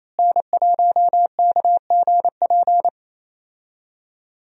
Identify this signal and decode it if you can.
Morse code, heard as a keyed tone, about 27 wpm, 700 Hz. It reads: N1KGP
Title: USA Call Signs - 5 Characters - Repeated 3x - 28wpm